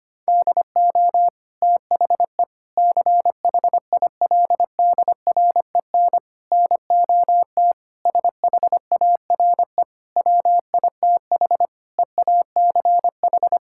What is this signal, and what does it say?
Morse code, heard as a keyed tone, about 25 wpm, 700 Hz. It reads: DO THE CHILDREN NOT SHARE WITH EACH
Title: Sentences from Top 1000 Words - Review Part 6 - 25wpm